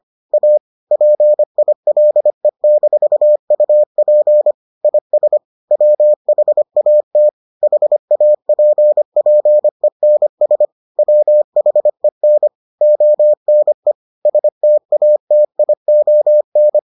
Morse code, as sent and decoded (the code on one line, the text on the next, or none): .- / .--. .. .-.. -....- ..- .--. / .. ... / .-- .... .- - / .... .- .--. .--. -. ... / .-- .... -. / --- -. / ... - .- - .. --- -.
A PILE-UP IS WHAT HAPPENS WHEN ONE STATION